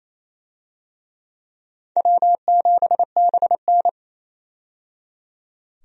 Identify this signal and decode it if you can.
Morse code, heard as a keyed tone, about 28 wpm, 700 Hz. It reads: W7BN
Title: USA Call Signs - 4 Characters - 28wpm